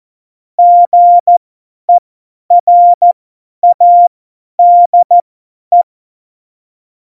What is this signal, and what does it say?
Morse code, heard as a keyed tone, about 14 wpm, spaced as for 10 wpm, 700 Hz. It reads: GERADE
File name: Morsekurs / Witze / Witz 07. Januar 2020, 50 F70